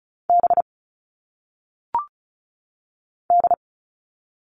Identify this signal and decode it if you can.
Morse code, heard as a keyed tone, about 36 wpm, 700 Hz. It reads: B D
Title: All Cut Numbers - 1-Digit 35wpm